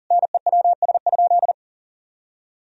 Morse code, heard as a keyed tone, about 40 wpm, 700 Hz. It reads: NEWS?
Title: Sentences from Top 1000 Words - Review Part 4 - 40wpm